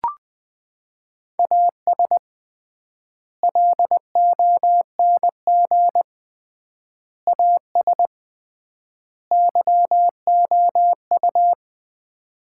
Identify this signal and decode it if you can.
Morse code, heard as a keyed tone, about 20 wpm, 700 Hz. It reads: AS LONG AS YOU
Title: 3x Word-Spacing - Sets of 4 Words - 20wpm